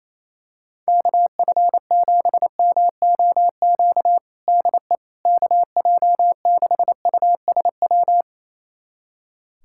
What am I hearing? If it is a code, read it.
Morse code, heard as a keyed tone, about 28 wpm, 700 Hz. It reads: KF7MOQ DE KJ6USW
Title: USA Callsign DE Callsign - 6 Characters - 28wpm